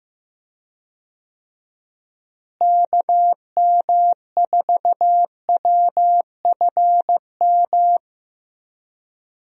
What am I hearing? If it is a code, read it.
Morse code, heard as a keyed tone, about 15 wpm, 700 Hz. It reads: KM4WFM
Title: USA Call Signs - 6 Characters - Repeated 3x - 15wpm